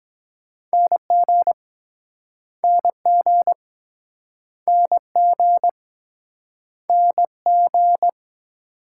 Morse code, heard as a keyed tone, about 24 wpm, 700 Hz. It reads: NG NG NG NG